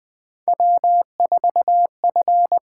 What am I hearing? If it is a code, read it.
Morse code, heard as a keyed tone, about 20 wpm, 700 Hz. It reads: W4F
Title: USA Call Signs - 4 Characters - 8x Character Spacing - 20wpm